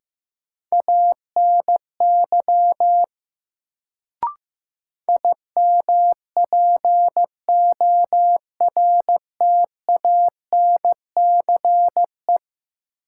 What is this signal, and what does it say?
Morse code, heard as a keyed tone, about 15 wpm, 700 Hz. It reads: ANY IMPORTANCE
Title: Sets of 3 Words - New Word Drill - 15wpm